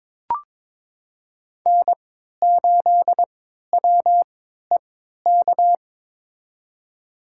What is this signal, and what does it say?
Morse code, heard as a keyed tone, about 22 wpm, spaced as for 12 wpm, 700 Hz. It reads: N8WEK
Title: USA Call Signs - 3x Character Spacing - 22wpm